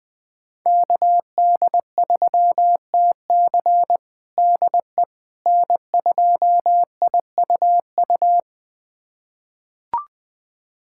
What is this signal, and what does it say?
Morse code, heard as a keyed tone, about 21 wpm, 700 Hz. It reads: KD3TC DE N2IUU
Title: USA Callsign DE Callsign - 5 Characters - 20wpm